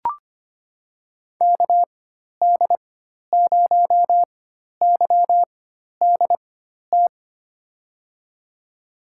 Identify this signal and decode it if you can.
Morse code, heard as a keyed tone, about 25 wpm, spaced as for 12 wpm, 700 Hz. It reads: KD0YDT
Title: USA Call Signs - 6 Characters - 4x Character Spacing - 25wpm